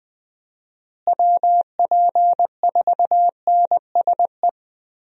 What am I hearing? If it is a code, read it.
Morse code, heard as a keyed tone, about 20 wpm, 700 Hz. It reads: WP4NSE
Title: USA Call Signs - 6 Characters - Repeated 2x - 20wpm